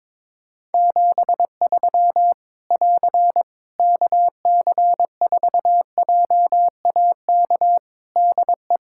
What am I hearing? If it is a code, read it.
Morse code, heard as a keyed tone, about 22 wpm, 700 Hz. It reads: 73 <AR> KC4JAK DE
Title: Practice QSO - Exchange 4 - Ending - 22wpm